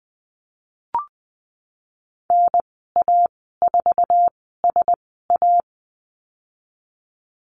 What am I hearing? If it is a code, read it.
Morse code, heard as a keyed tone, about 19 wpm, spaced as for 14 wpm, 700 Hz. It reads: NA4SA
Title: 048 Lesson 9g - Number 4 - Call Signs 20wpm